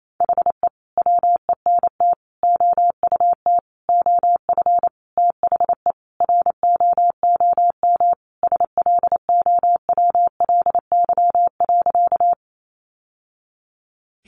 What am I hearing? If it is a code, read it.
Morse code, heard as a keyed tone, about 28 wpm, 700 Hz. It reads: HE WENT OUT OF THE ROOM SLOWLY.
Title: Sentences from Top 1100 Words 28wpm